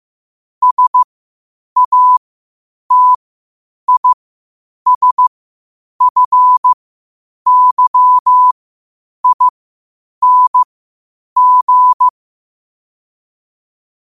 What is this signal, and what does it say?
Morse code, heard as a keyed tone, about 15 wpm, spaced as for 8 wpm, 1000 Hz. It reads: SATISFYING
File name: qotdfiles / 8wpm